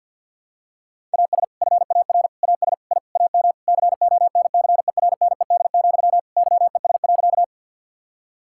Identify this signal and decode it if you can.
Morse code, heard as a keyed tone, about 47 wpm, 700 Hz. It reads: AS FAR AS I AM CONCERNED, YES.
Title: Sentences from Top 1000 Words - Review Part 6 - 50wpm